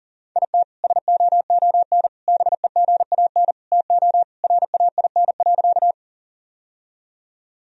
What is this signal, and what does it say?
Morse code, heard as a keyed tone, about 40 wpm, 700 Hz. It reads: IT SOON BEGAN TO RAIN.
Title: Sentences from Top 1000 Words - Review Part 1 - 40wpm